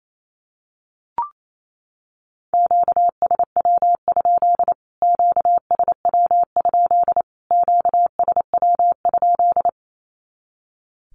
Morse code, heard as a keyed tone, about 28 wpm, 700 Hz. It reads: QSW? QSW? QSW?
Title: Q-Codes Repeated 3x 28wpm